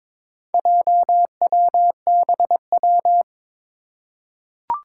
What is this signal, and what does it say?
Morse code, heard as a keyed tone, about 23 wpm, 700 Hz. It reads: JWBW
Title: Four Challenging Letter-Number Combinations 22wpm